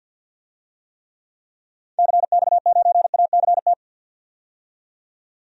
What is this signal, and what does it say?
Morse code, heard as a keyed tone, about 46 wpm, 700 Hz. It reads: CX9ACT
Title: DXCC-100 Call Signs - 6 Characters - 50wpm